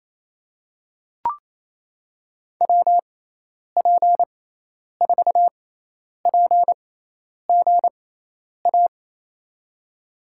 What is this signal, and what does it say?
Morse code, heard as a keyed tone, about 27 wpm, spaced as for 10 wpm, 700 Hz. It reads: WP4PGA
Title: USA Call Signs - 6 Characters - 6x Character Spacing - 28wpm